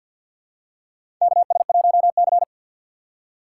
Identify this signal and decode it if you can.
Morse code, heard as a keyed tone, about 47 wpm, 700 Hz. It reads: KS1C
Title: USA Call Signs - 6x Character Spacing - 50wpm